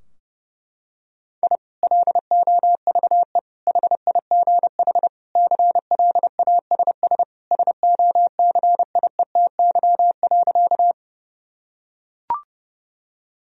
Morse code, heard as a keyed tone, about 30 wpm, 700 Hz. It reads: I LOVE HIGH CLASS SOCIETY.
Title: Sentences from Top 1000 Words - Review Part 4 - 30wpm